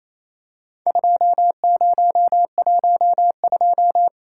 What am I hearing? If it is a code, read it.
Morse code, heard as a keyed tone, about 28 wpm, 700 Hz. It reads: 2012
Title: Worked All Europe DX Contest - QTC Exchange - 28wpm